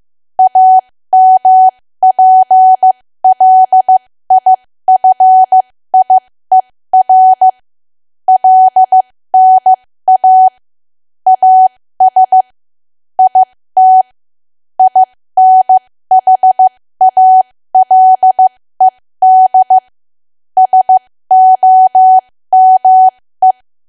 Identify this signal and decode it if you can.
Morse code, heard as a keyed tone, about 15 wpm, spaced as for 13 wpm, 750 Hz. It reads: AMPLIFIER LNA AS IT INHALED SOME
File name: cwa / W1AW-Level2 / 13WPM